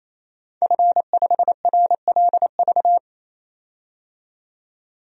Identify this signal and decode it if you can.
Morse code, heard as a keyed tone, about 28 wpm, 700 Hz. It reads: F5RLV
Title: DXCC-100 Call Signs - 5 Characters - 28wpm